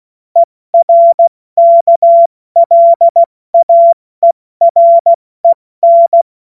Morse code, heard as a keyed tone, about 16 wpm, 680 Hz. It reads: ERKLAEREN
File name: Morsekurs / Witze / Witz 01. August 2023 70 F80